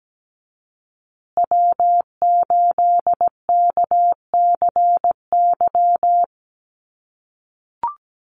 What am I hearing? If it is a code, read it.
Morse code, heard as a keyed tone, about 17 wpm, 700 Hz. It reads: W8KCY
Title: USA Call Signs Repeated 2x 17wpm